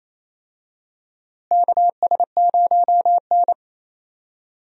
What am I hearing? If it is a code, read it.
Morse code, heard as a keyed tone, about 28 wpm, 700 Hz. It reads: KS0N
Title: USA Call Signs - 4 Characters - Repeated 2x - 28wpm